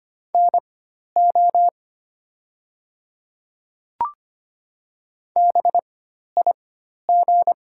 Morse code, heard as a keyed tone, about 25 wpm, spaced as for 12 wpm, 700 Hz. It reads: NO BIG
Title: 4x Character Spacing - Top 200 Words - 25wpm